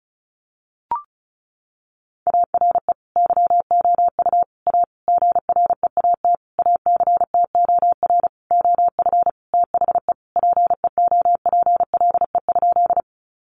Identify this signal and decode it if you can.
Morse code, heard as a keyed tone, about 34 wpm, 700 Hz. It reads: ARE YOU A GREAT ACTOR OF THE PEOPLE?
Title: Sentences from Top 1500 Words 35wpm